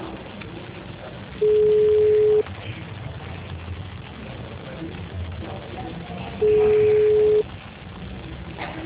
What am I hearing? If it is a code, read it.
Ringback tone (Europe, 425 Hz).
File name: out- -2007- - - .34890